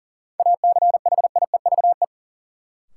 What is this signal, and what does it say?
Morse code, heard as a keyed tone, about 40 wpm, 700 Hz. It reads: ACHIEVE
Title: Sentences from Top 1000 Words - Review Part 5 - New Word Drill - 40wpm